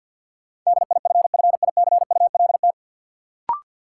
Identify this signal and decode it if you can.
Morse code, heard as a keyed tone, about 53 wpm, 700 Hz. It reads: DIFFICULT
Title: Sets of 4 Words - New Word Drill - 50wpm